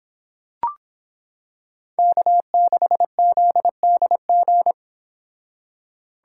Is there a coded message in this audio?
Yes: Morse code from 2.0 to 4.7 s, heard as a keyed tone, about 25 wpm, 700 Hz. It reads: K6ZDG